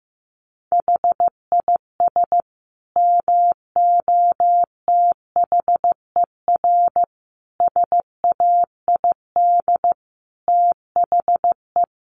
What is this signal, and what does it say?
Morse code, heard as a keyed tone, about 15 wpm, 700 Hz. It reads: HIS MOTHER SAID THE
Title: Sentences from Top 800 Words 15wpm